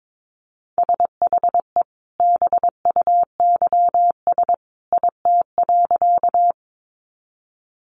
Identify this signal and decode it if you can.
Morse code, heard as a keyed tone, about 22 wpm, 700 Hz. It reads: SHE BUYS IT.
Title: Additional Easy Phrases 22wpm